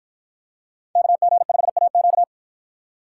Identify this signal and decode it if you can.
Morse code, heard as a keyed tone, about 52 wpm, 700 Hz. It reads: KG5AQ